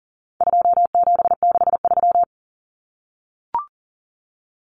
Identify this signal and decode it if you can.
Morse code, heard as a keyed tone, about 40 wpm, 700 Hz. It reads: 2763
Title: Common Cut Numbers - 4-Digit 40wpm